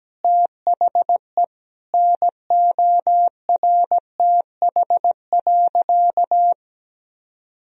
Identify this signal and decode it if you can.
Morse code, heard as a keyed tone, about 17 wpm, 700 Hz. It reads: THE NORTH.
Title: Sentences from Top 1000 Words - Review Part 1 - 17wpm